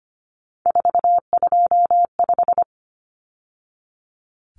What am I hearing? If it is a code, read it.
Morse code, heard as a keyed tone, about 25 wpm, 700 Hz. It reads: 425
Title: Numbers - 3-Digit 25wpm